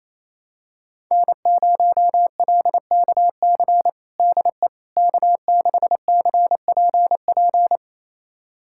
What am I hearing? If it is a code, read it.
Morse code, heard as a keyed tone, about 28 wpm, 700 Hz. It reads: N0LKC DE K6CPP
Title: USA Callsign DE Callsign - 5 Characters - 28wpm